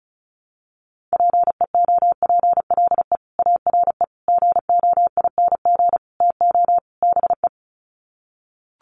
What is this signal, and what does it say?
Morse code, heard as a keyed tone, about 34 wpm, 700 Hz. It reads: PEOPLE ARE GOING TO BE
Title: Sets of 5 Words Encore 35wpm